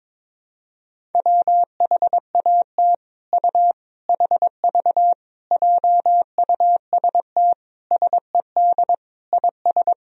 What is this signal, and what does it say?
Morse code, heard as a keyed tone, about 22 wpm, 700 Hz. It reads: WHAT U HV JUST SED IS
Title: Sentences from Top 200 Words and Abbreviations 22wpm